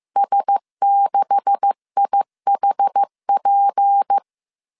Morse code, heard as a keyed tone, about 15 wpm, 800 Hz. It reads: S6IHP
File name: CW / DET / 1 9 1500C S A 14wpm 3222